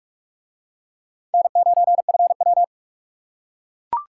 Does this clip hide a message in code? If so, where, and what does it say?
Yes: Morse code from 1.3 to 2.6 s, heard as a keyed tone, about 40 wpm, 700 Hz. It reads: N9FW